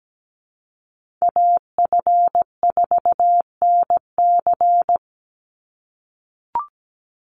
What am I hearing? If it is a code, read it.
Morse code, heard as a keyed tone, about 17 wpm, 700 Hz. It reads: AF4NC
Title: Call Signs - 17wpm